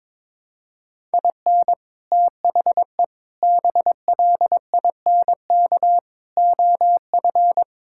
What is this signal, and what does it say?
Morse code, heard as a keyed tone, about 22 wpm, 700 Hz. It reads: IN THE BLINK OF
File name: practice / 22wpm / Sets of 6 Words - 22wpm